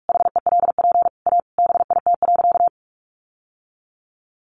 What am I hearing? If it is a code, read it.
Morse code, heard as a keyed tone, about 44 wpm, 700 Hz. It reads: HELP A BIT.
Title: Sentences from Top 900 Words 45wpm